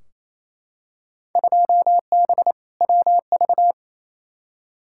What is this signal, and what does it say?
Morse code, heard as a keyed tone, about 28 wpm, 700 Hz. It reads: 2B WV
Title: Field Day Exchanges Encore - 28wpm